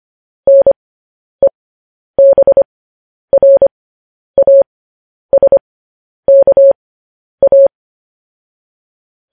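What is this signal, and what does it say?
Morse code, heard as a keyed tone, about 25 wpm, spaced as for 10 wpm, 550 Hz. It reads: NEBRASKA
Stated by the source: Morse code news headlines broadcast.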